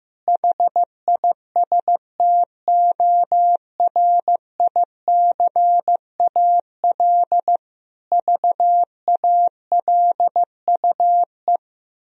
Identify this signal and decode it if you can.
Morse code, heard as a keyed tone, about 15 wpm, 700 Hz. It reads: HISTORICAL VALUE
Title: Sentences from Top 1000 Words - Review Part 4 - 15wpm